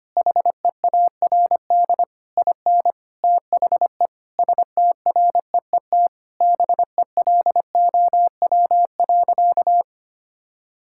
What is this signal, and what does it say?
Morse code, heard as a keyed tone, about 25 wpm, 700 Hz. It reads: HEARD IN THE STREET BELOW.
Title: Sentences from Top 1000 Words - Review Part 5 - 25wpm